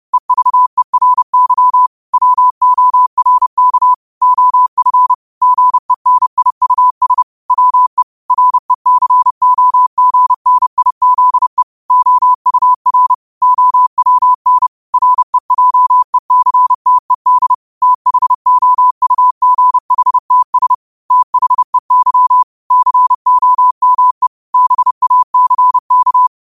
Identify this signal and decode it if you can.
Morse code, heard as a keyed tone, about 30 wpm, 1000 Hz. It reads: EVERY WORK OF GENIUS WE RECOGNIZE OUR OWN REJECTED THOUGHTS THEY COME BACK